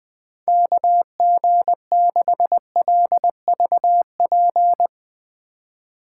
Morse code, heard as a keyed tone, about 21 wpm, 700 Hz. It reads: KG6LVP
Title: Speed Racing - USA 2x3 Call Signs 20wpm